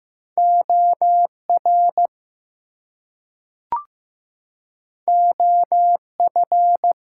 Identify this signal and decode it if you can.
Morse code, heard as a keyed tone, about 15 wpm, 700 Hz. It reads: OR OF
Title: Two Letter Words - 15wpm